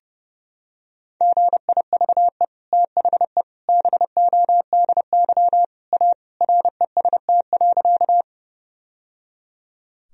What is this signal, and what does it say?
Morse code, heard as a keyed tone, about 30 wpm, 700 Hz. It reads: GIVE THE BODY A REST.
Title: Sentences from Top 1000 Words - Review Part 2 - 30wpm